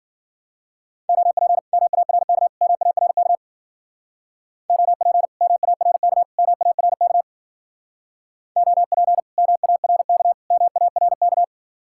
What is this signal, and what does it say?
Morse code, heard as a keyed tone, about 49 wpm, 700 Hz. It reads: OP MARK MARK OP MARK MARK OP MARK MARK
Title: Speed-Racing Common QSO Phrases Encore - 40wpm